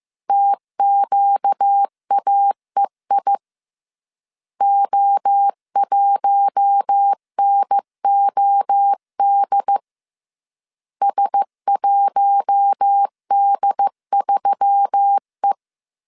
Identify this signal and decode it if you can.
Morse code, heard as a keyed tone, about 15 wpm, 800 Hz. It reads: TQAEI O1NOD S1D3E
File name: CW / DET / 1 6 1500C S C 14wpm 2951